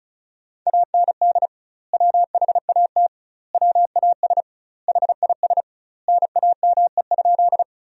Morse code, heard as a keyed tone, about 34 wpm, 700 Hz. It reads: AND WHAT WAS HIS NAME?
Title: 2x Word-Spacing - Sentences from Top 200 Words - 35wpm